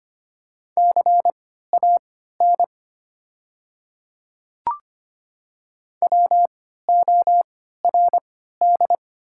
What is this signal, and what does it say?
Morse code, heard as a keyed tone, about 25 wpm, spaced as for 14 wpm, 700 Hz. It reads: CAN WORD
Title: 3x Character Spacing - Top 100 Words - 25wpm